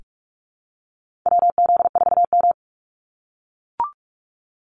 Morse code, heard as a keyed tone, about 44 wpm, 700 Hz. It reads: WZ4M
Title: NA SOTA Activator Callsigns 45wpm